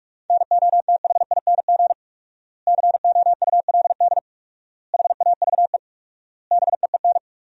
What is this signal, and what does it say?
Morse code, heard as a keyed tone, about 46 wpm, 700 Hz. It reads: NOTHING COULD HAVE BEEN
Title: 4x Word-Spacing - Sentences from Top 500 Words - 45wpm